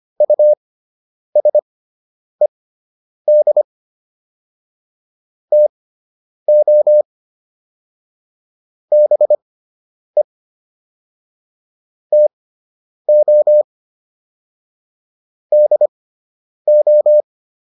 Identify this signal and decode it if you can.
Morse code, heard as a keyed tone, about 25 wpm, spaced as for 9 wpm, 600 Hz. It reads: USED TO BE TO DO